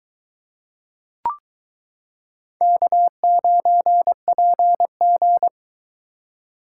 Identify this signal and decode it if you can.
Morse code, heard as a keyed tone, about 22 wpm, 700 Hz. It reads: K9PG